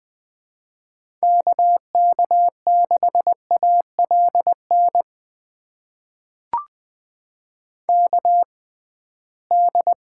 Morse code, heard as a keyed tone, about 20 wpm, spaced as for 15 wpm, 700 Hz. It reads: KK6ALN KD
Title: USA Call Signs - 6 Characters - 6x Character Spacing - 20wpm